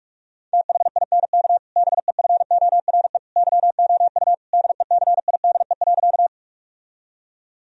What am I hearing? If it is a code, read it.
Morse code, heard as a keyed tone, about 46 wpm, 700 Hz. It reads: THINK BEFORE YOU DECIDE.
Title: Sentences from Top 1000 Words - Review Part 3 - 45wpm